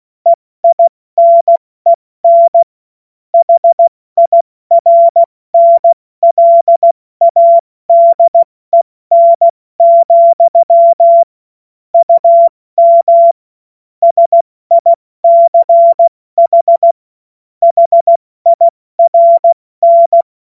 Morse code, heard as a keyed tone, about 16 wpm, 680 Hz. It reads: EINEN HIRNLADEN, UM SICH HIRN
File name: Morsekurs / Witze / Witz 07. Juni 2022 70 F80